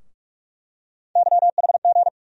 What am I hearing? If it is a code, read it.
Morse code, heard as a keyed tone, about 44 wpm, 700 Hz. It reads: YHG
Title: USA Call Sign Postfixes - 3 Letters 45wpm